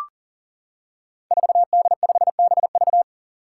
Morse code, heard as a keyed tone, about 40 wpm, 700 Hz. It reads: 4D5BV
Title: Five Challenging Letter-Number Combinations 40wpm